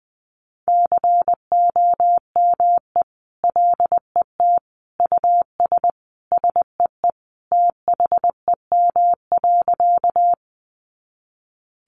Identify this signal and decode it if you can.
Morse code, heard as a keyed tone, about 20 wpm, 700 Hz. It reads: COME LET US SEE THEM.
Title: Sentences from Top 300 Words 20wpm